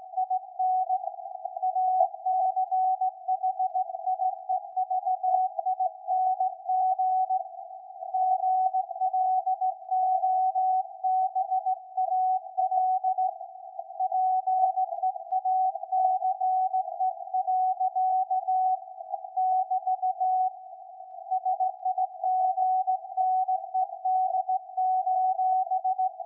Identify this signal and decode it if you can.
Morse code, heard as a keyed tone, about 17 wpm, 750 Hz. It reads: IN ACHIEVING GLOBAL PEACE. <BT> SIGNED:
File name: iz7slz / VLF / SAQ 13-02-2023